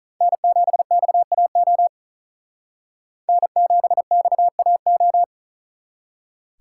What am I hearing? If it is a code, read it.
Morse code, heard as a keyed tone, about 43 wpm, 700 Hz. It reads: N7XAO N7XAO
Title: Speed Racing - USA 1x3 Call Signs 35wpm